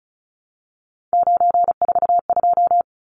Morse code, heard as a keyed tone, about 34 wpm, 700 Hz. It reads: 942
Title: Numbers - 3-Digit 35wpm